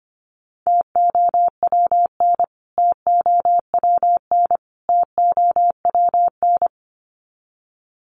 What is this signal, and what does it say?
Morse code, heard as a keyed tone, about 25 wpm, 700 Hz. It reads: TOWN TOWN TOWN
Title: Top 500 Words Repeated 25wpm